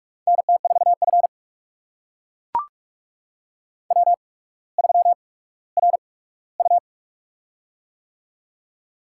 Morse code, heard as a keyed tone, about 44 wpm, spaced as for 19 wpm, 700 Hz. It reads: NT4F W3RU
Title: USA Call Signs - 4 Characters - 8x Character Spacing - 45wpm